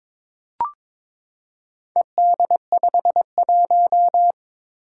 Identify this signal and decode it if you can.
Morse code, heard as a keyed tone, about 22 wpm, 700 Hz. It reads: ED51
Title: Four Challenging Letter-Number Combinations 22wpm